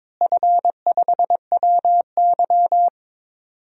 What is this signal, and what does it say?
Morse code, heard as a keyed tone, about 22 wpm, 700 Hz. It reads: F5WY
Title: DXCC-100 Call Signs - 4 Characters - Repeated 3x - 22wpm